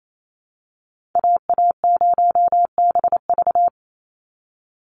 Morse code, heard as a keyed tone, about 28 wpm, 700 Hz. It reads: AA0BV
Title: NA SOTA Activator Callsigns 28wpm